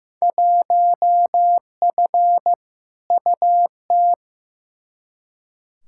Morse code, heard as a keyed tone, about 15 wpm, 700 Hz. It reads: 1F UT
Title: Field Day Exchanges Encore - 15wpm